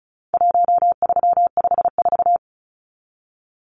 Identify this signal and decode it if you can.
Morse code, heard as a keyed tone, about 34 wpm, 700 Hz. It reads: 1354
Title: Numbers - 4-Digit 35wpm